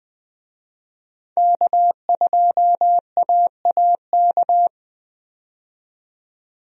Morse code, heard as a keyed tone, about 20 wpm, 700 Hz. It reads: K2AAK
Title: USA Call Signs - 5 Characters - 20wpm